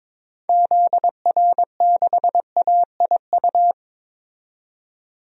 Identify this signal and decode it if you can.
Morse code, heard as a keyed tone, about 23 wpm, 700 Hz. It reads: ZR6AIU
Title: DXCC-100 Call Signs - 6 Characters - 22wpm